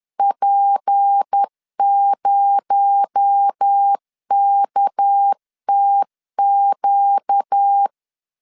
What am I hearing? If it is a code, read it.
Morse code, heard as a keyed tone, about 10 wpm, 800 Hz. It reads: P0KTQ